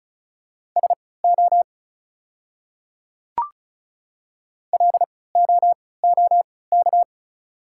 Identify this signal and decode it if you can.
Morse code, heard as a keyed tone, about 34 wpm, spaced as for 20 wpm, 690 Hz. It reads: SO LOOK
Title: 3x Character Spacing - Top 100 Words - 35wpm